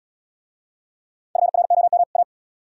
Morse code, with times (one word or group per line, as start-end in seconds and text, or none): 1.4-2.0 FROM
2.2-2.2 A